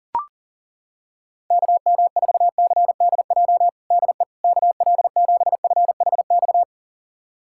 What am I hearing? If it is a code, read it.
Morse code, heard as a keyed tone, about 40 wpm, 700 Hz. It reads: KM4CDJ DE KL7FHX
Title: USA Callsign DE Callsign - 6 Characters - 40wpm